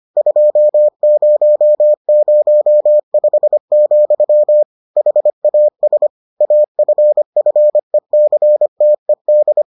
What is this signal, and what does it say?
Morse code, heard as a keyed tone, about 25 wpm, 600 Hz. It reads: 2005, HAS AFFECTED